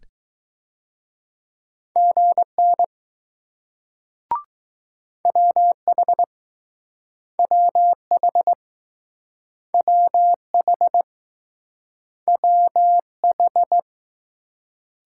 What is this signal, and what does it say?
Morse code, heard as a keyed tone, about 20 wpm, 700 Hz. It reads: GN WH WH WH WH